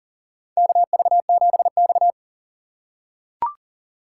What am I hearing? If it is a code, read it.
Morse code, heard as a keyed tone, about 40 wpm, 700 Hz. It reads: KV7X
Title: USA Call Signs - 4x Character Spacing - 40wpm